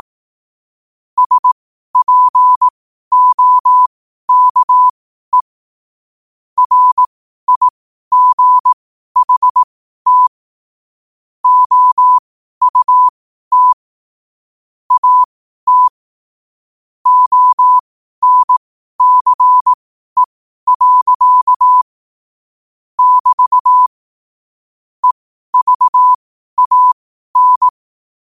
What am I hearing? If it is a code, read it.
Morse code, heard as a keyed tone, about 18 wpm, spaced as for 12 wpm, 1000 Hz. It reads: SPOKE RIGHT OUT AT ONCE. <BT> EVAN
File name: qotdfiles / 12wpm